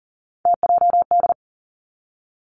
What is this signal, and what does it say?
Morse code, heard as a keyed tone, about 42 wpm, 700 Hz. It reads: TJD
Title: Three Challenging Letter-Number Combinations 40wpm